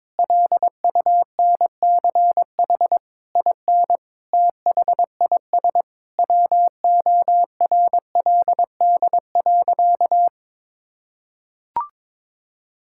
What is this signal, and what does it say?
Morse code, heard as a keyed tone, about 22 wpm, 700 Hz. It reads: LUNCH IN THIS WORLD.
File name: Sentences with 9-Word Phrases - 22wpm